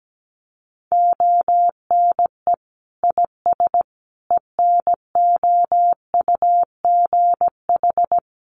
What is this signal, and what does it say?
Morse code, heard as a keyed tone, about 17 wpm, 700 Hz. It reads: ONE IS ENOUGH
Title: Sentences from Top 1100 Words 17wpm